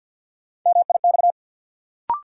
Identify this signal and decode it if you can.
Morse code, heard as a keyed tone, about 50 wpm, 700 Hz. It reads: MIX
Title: Sentences from Top 1000 Words - Review Part 4 - New Word Drill - 50wpm